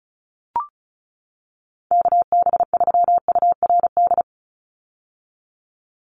Morse code, heard as a keyed tone, about 34 wpm, 700 Hz. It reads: KB3URD
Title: USA Call Signs 35wpm